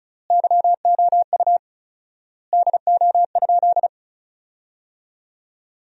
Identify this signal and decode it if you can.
Morse code, heard as a keyed tone, about 34 wpm, 700 Hz. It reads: YOU DO?
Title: 4x Word-Spacing - Sentences from Top 100 Words - 35wpm